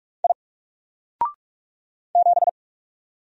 Morse code, heard as a keyed tone, about 44 wpm, 700 Hz. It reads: I 7
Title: Single Character (Letter:Number:Punctuation:Prosign) 45wpm